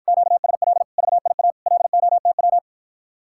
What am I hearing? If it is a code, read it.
Morse code, heard as a keyed tone, about 54 wpm, 700 Hz. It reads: QSL VIA LOTW